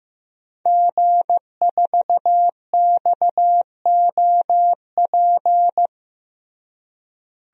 Morse code, heard as a keyed tone, about 15 wpm, 700 Hz. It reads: G4XOP
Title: DXCC-100 Call Signs - 5 Characters - Repeated 2x - 15wpm